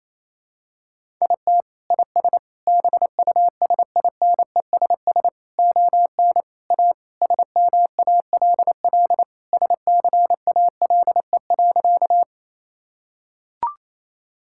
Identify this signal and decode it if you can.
Morse code, heard as a keyed tone, about 28 wpm, 700 Hz. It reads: IT IS BUSINESS ON A SMALL SCALE.
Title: Sentences from Top 1000 Words - Review Part 5 - 28wpm